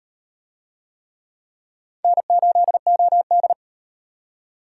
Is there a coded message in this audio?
Yes: Morse code from 2.0 to 3.5 s, heard as a keyed tone, about 36 wpm, 700 Hz. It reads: N8OD